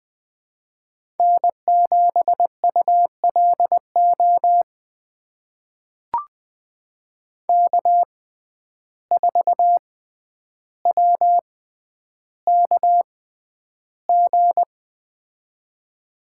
Morse code, heard as a keyed tone, about 20 wpm, spaced as for 10 wpm, 700 Hz. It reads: N7ULO K4WKG
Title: USA Call Signs - 5 Characters - 6x Character Spacing - 20wpm